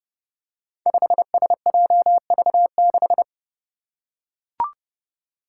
Morse code, heard as a keyed tone, about 30 wpm, 700 Hz. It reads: 5SJV6
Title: Five Challenging Letter-Number Combinations 30wpm